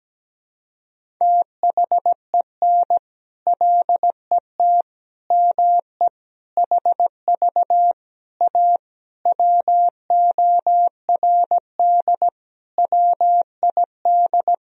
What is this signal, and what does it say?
Morse code, heard as a keyed tone, about 17 wpm, 700 Hz. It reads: THEN LET ME HV A WORD WID
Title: Sentences from Top 300 Words and Abbreviations 17wpm